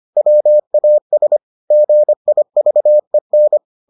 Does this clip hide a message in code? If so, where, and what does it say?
Yes: Morse code from 0.2 to 3.6 s, heard as a keyed tone, about 25 wpm, 600 Hz. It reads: WAS GIVEN